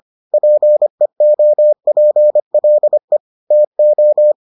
Morse code, heard as a keyed tone, about 25 wpm, 600 Hz. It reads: PEOPLE TO